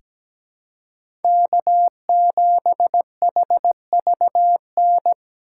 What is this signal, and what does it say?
Morse code, heard as a keyed tone, about 17 wpm, 700 Hz. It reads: K7HVN
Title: USA Call Signs - 5 Characters - 4x Character Spacing - 17wpm